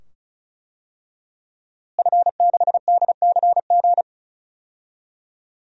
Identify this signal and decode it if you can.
Morse code, heard as a keyed tone, about 36 wpm, 700 Hz. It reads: F6DCG
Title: DXCC-100 Call Signs - 5 Characters - Repeated 2x - 35wpm